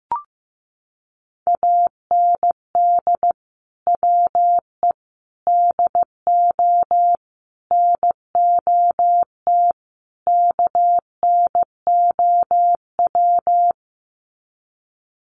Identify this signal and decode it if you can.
Morse code, heard as a keyed tone, about 14 wpm, 700 Hz. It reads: AND WE DO NOT KNOW
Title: Sets of 5 Words Encore 15wpm